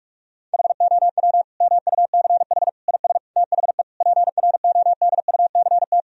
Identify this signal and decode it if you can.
Morse code, heard as a keyed tone, about 46 wpm, 700 Hz. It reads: HOW MUCH IS THE PRODUCT
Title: Sentences from Top 1000 Words - Review Part 2 - 45wpm